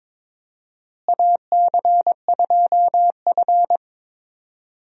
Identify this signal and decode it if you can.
Morse code, heard as a keyed tone, about 22 wpm, 700 Hz. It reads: AC2F
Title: USA Call Signs - 4 Characters - Repeated 3x - 22wpm